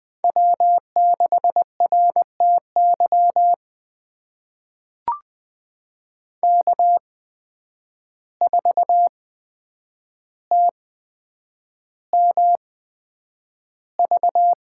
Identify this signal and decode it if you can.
Morse code, heard as a keyed tone, about 20 wpm, spaced as for 9 wpm, 700 Hz. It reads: W6RTY K4TMV
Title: USA Call Signs - 5 Characters - 8x Character Spacing - 20wpm